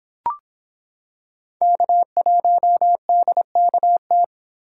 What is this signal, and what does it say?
Morse code, heard as a keyed tone, about 26 wpm, 700 Hz. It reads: K1DKT